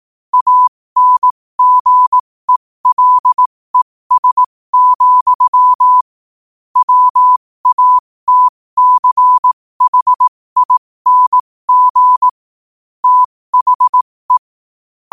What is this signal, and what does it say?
Morse code, heard as a keyed tone, about 18 wpm, spaced as for 15 wpm, 1000 Hz. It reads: ANGELES, WATCHING THE
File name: qotdfiles / 15wpm